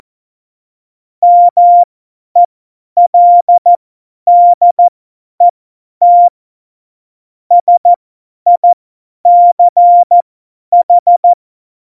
Morse code, heard as a keyed tone, about 14 wpm, spaced as for 10 wpm, 700 Hz. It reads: MELDET SICH
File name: Morsekurs / Witze / Witz 11. Juni 2019, 50 F70